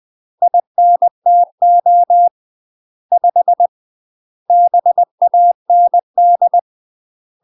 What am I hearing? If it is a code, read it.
Morse code, heard as a keyed tone, about 20 wpm, 700 Hz. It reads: INTO 5 BAND
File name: UK MASTER QSOs TRAINING / 20WPM / UK MASTER QSOs 0045